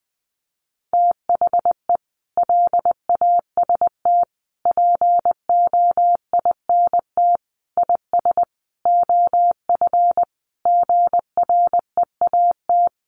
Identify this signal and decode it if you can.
Morse code, heard as a keyed tone, about 20 wpm, 700 Hz. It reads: THE LAST POINT IS OF GREAT
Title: Sentences from Top 1200 Words 20wpm